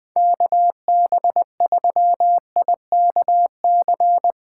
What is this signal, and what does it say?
Morse code, heard as a keyed tone, about 21 wpm, 700 Hz. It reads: KB3IKC
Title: USA Callsign DE Callsign - 6 Characters - 20wpm